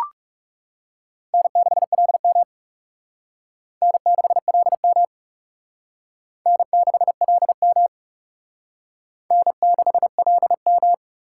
Morse code, heard as a keyed tone, about 36 wpm, 700 Hz. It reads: N6LM N6LM N6LM N6LM
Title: Speed Racing - USA 1x2 Call Signs 30wpm